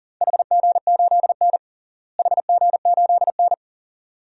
Morse code, heard as a keyed tone, about 40 wpm, 700 Hz. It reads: HG8N HG8N
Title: DXCC-100 Call Signs - 4 Characters - Repeated 2x - 40wpm